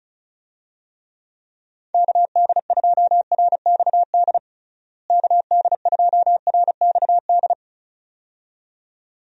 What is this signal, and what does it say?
Morse code, heard as a keyed tone, about 34 wpm, 700 Hz. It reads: KD2RXD KD2RXD
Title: USA Call Signs - 6 Characters - Repeated 2x - 35wpm